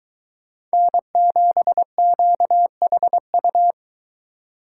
Morse code, heard as a keyed tone, about 22 wpm, 700 Hz. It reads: N7QHU